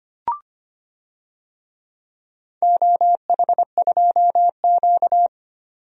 Morse code, heard as a keyed tone, about 25 wpm, 700 Hz. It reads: OH2Q
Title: DXCC-100 Call Signs - 4 Characters - 25wpm